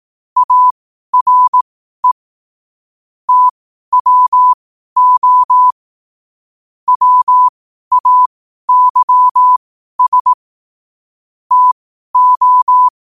Morse code, heard as a keyed tone, about 18 wpm, spaced as for 12 wpm, 1000 Hz. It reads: ARE TWO WAYS TO
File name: qotdfiles / 12wpm